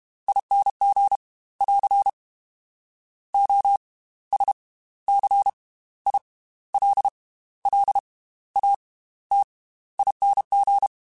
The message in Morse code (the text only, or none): ING <AR> OSCILLATING